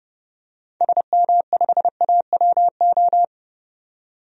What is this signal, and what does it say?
Morse code, heard as a keyed tone, about 30 wpm, 700 Hz. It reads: SM5AWO
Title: DXCC-100 Call Signs - 6 Characters - 30wpm